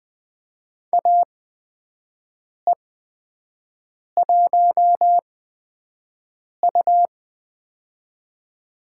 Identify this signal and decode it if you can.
Morse code, heard as a keyed tone, about 20 wpm, spaced as for 5 wpm, 700 Hz. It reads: AE1U
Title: USA Call Signs - 4 Characters - 8x Character Spacing - 20wpm